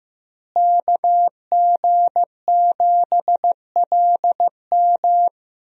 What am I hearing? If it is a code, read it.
Morse code, heard as a keyed tone, about 15 wpm, 700 Hz. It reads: KG7LM
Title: USA Call Signs - 5 Characters - Repeated 2x - 15wpm